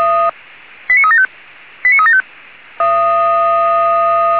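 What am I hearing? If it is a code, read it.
Five-tone selective call (ZVEI), heard as quick runs of five tones, tones of about 70 ms. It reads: 89375 89375
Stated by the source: Five-tone ZVEI selective call received on an SDR.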